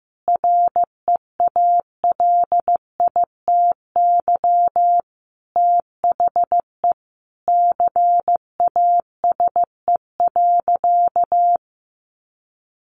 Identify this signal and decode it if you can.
Morse code, heard as a keyed tone, about 15 wpm, 700 Hz. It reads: REALITY THE CASE.
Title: Sentences from Top 1200 Words 15wpm